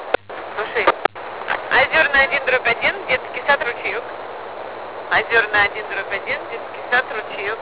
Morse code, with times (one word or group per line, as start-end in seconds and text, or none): none